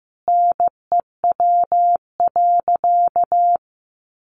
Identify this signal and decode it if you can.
Morse code, heard as a keyed tone, about 15 wpm, 700 Hz. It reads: NEW.
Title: Sentences from Top 1300 Words 15wpm